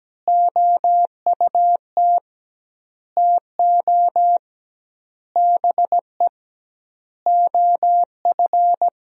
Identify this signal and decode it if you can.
Morse code, heard as a keyed tone, about 17 wpm, 700 Hz. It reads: OUT TO BE OF
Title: 2x Word-Spacing - Sentences from Top 400 Words - 17wpm